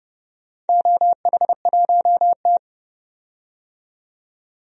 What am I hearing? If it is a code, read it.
Morse code, heard as a keyed tone, about 30 wpm, 700 Hz. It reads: OH1T
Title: DXCC-100 Call Signs - 4 Characters - 30wpm